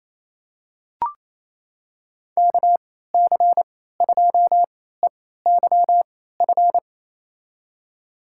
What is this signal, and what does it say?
Morse code, heard as a keyed tone, about 28 wpm, spaced as for 16 wpm, 700 Hz. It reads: KC2EYF
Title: USA Call Signs - 6 Characters - 3x Character Spacing - 28wpm